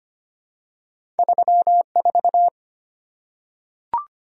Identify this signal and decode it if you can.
Morse code, heard as a keyed tone, about 25 wpm, 700 Hz. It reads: 34
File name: Two Letter-Number Combinations - 25wpm